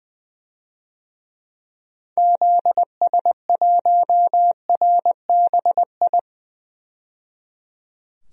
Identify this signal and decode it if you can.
Morse code, heard as a keyed tone, about 20 wpm, 700 Hz. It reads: ZS1RBI
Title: DXCC-100 Call Signs - 6 Characters - 20wpm